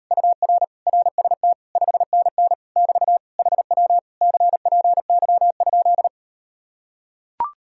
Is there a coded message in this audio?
Yes: Morse code from 0.1 to 6.1 s, heard as a keyed tone, about 38 wpm, 700 Hz. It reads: UR RST 5NN <BT> HW CPY?